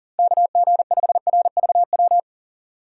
Morse code, heard as a keyed tone, about 40 wpm, 700 Hz. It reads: KG5RVW
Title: USA Call Signs - 6 Characters - 3x Character Spacing - 40wpm